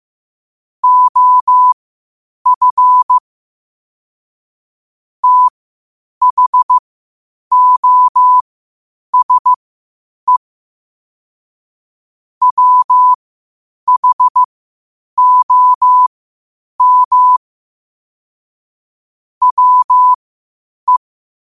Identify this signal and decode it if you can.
Morse code, heard as a keyed tone, about 15 wpm, spaced as for 8 wpm, 1000 Hz. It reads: OF THOSE WHOM WE
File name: qotdfiles / 8wpm